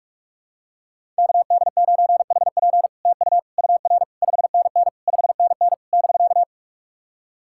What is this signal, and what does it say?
Morse code, heard as a keyed tone, about 44 wpm, 700 Hz. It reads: KD9HP TU UR 5NN 5NN <BK>
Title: SOTA QSO Practice 45wpm